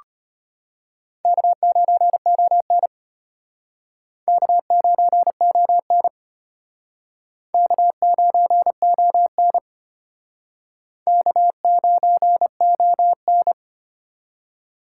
Morse code, heard as a keyed tone, about 34 wpm, 700 Hz. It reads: K9ON K9ON K9ON K9ON